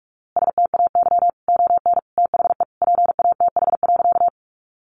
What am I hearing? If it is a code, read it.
Morse code, heard as a keyed tone, about 44 wpm, 700 Hz. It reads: STAY ON THE PATH.
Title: A Sound Of Thunder 45wpm